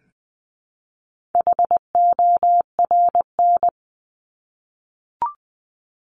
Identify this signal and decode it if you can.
Morse code, heard as a keyed tone, about 20 wpm, 700 Hz. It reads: HORN
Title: 260 Lesson 42e - Words 20-20wpm